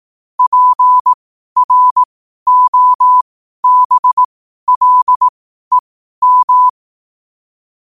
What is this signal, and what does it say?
Morse code, heard as a keyed tone, about 18 wpm, spaced as for 12 wpm, 1000 Hz. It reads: PROBLEM
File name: qotdfiles / 12wpm